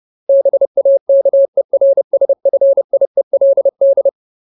Morse code, heard as a keyed tone, about 30 wpm, 550 Hz. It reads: BAKERSFIELD
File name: Ultimate Word List / 30WPM Ultimate